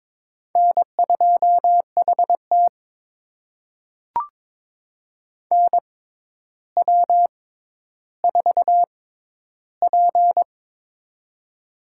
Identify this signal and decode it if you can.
Morse code, heard as a keyed tone, about 22 wpm, spaced as for 11 wpm, 700 Hz. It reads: N2HT NW4P
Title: USA Call Signs - 4 Characters - 6x Character Spacing - 22wpm